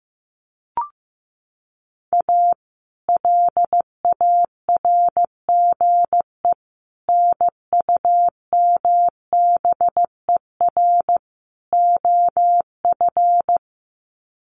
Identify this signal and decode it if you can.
Morse code, heard as a keyed tone, about 15 wpm, 700 Hz. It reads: A LARGE NUMBER OF
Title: Sets of 4 Words 15wpm